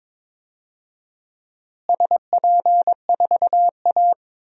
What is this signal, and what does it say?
Morse code, heard as a keyed tone, about 22 wpm, 700 Hz. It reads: SP4A
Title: DXCC-100 Call Signs - 4 Characters - Repeated 3x - 22wpm